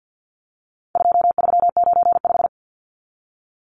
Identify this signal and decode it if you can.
Morse code, heard as a keyed tone, about 50 wpm, 700 Hz. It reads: 2395
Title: Numbers - 4-Digit 50wpm